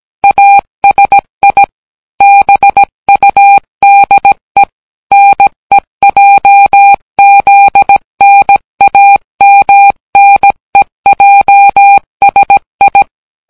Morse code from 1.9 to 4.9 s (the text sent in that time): BUDE